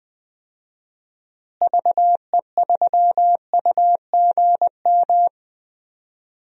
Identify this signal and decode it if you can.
Morse code, heard as a keyed tone, about 20 wpm, 700 Hz. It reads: VE3UGM
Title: DXCC-100 Call Signs - 6 Characters - Repeated 2x - 20wpm